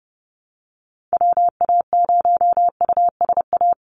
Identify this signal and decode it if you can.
Morse code, heard as a keyed tone, about 30 wpm, 700 Hz. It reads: WA0USA
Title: Call Signs Repeated 3x - 30wpm